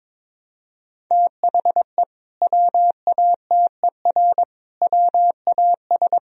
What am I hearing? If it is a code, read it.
Morse code, heard as a keyed tone, about 21 wpm, 700 Hz. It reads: THE WATER WAS
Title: Sentences from Top 1000 Words - Review Part 3 - 22wpm